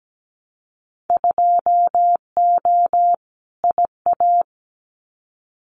Morse code, heard as a keyed tone, about 17 wpm, 700 Hz. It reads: 2O IA
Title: Winter Field Day Exchanges 17wpm